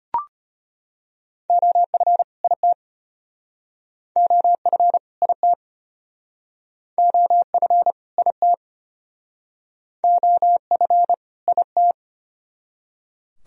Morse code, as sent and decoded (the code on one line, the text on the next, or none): --- ..-. / .. - / --- ..-. / .. - / --- ..-. / .. - / --- ..-. / .. -
OF IT OF IT OF IT OF IT